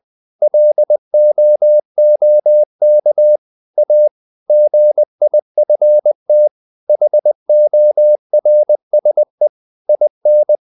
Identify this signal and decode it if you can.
Morse code, heard as a keyed tone, about 20 wpm, 600 Hz. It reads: LOOK A GIFT HORSE IN
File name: CW / Practice / EnglishSayings20